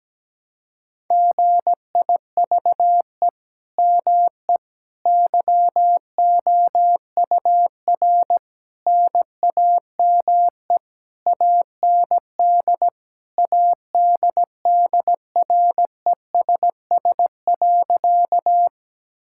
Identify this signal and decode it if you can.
Morse code, heard as a keyed tone, about 17 wpm, 700 Hz. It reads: GIVE ME YOUR NAME AND ADDRESS.
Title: Sentences from Top 1000 Words - Review Part 6 - 17wpm